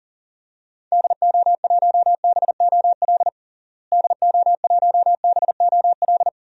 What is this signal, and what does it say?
Morse code, heard as a keyed tone, about 40 wpm, 700 Hz. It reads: DO1BOL DO1BOL
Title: DXCC-100 Call Signs - 6 Characters - Repeated 2x - 40wpm